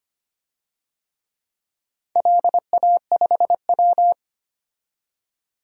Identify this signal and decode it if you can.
Morse code, heard as a keyed tone, about 25 wpm, 700 Hz. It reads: LA5W
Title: DXCC-100 Call Signs - 4 Characters - 25wpm